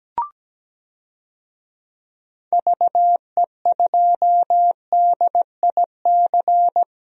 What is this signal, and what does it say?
Morse code, heard as a keyed tone, about 17 wpm, 700 Hz. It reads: VE2DIC
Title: DXCC-100 Call Signs - 6 Characters - 17wpm